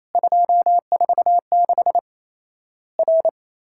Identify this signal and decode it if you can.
Morse code, heard as a keyed tone, about 28 wpm, 700 Hz. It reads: 246 R
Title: Worked All Europe DX Contest - QTC Exchange - 28wpm